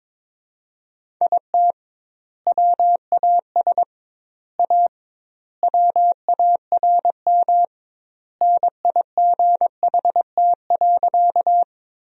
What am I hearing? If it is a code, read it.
Morse code, heard as a keyed tone, about 23 wpm, 700 Hz. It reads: IT WAS A WARM NIGHT.
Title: 2x Word-Spacing - Sentences from Top 500 Words - 22wpm